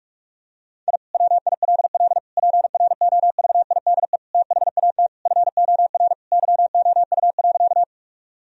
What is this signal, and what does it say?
Morse code, heard as a keyed tone, about 48 wpm, 700 Hz. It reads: I WILL PROVIDE THAT FOR YOU.
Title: Sentences from Top 1000 Words - Review Part 3 - 45wpm